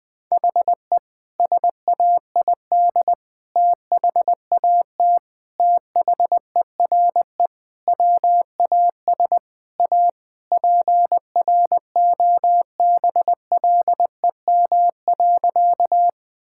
Morse code, heard as a keyed tone, about 20 wpm, 700 Hz. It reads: HE SAID THAT THERE WAS A PROBLEM.
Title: Sentences with 3-Word Phrases - Spoken Beginning - 20wpm